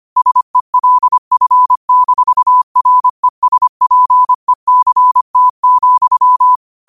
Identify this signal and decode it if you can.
Morse code, heard as a keyed tone, about 25 wpm, 1000 Hz. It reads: SELF-RESPECT,
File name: qotdfiles / 25wpm